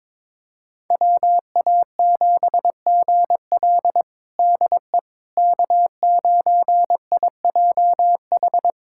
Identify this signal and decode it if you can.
Morse code, heard as a keyed tone, about 21 wpm, 700 Hz. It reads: WA7GL DE K9IJH
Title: USA Callsign DE Callsign - 5 Characters - 22wpm